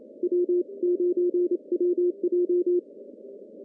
Morse code, heard as a keyed tone, about 28 wpm, 350 Hz. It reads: W9WJ
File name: CW / cqww-cw-k5tr